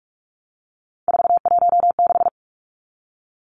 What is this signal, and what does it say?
Morse code, heard as a keyed tone, about 44 wpm, 700 Hz. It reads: 416
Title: Numbers - 3-Digit 45wpm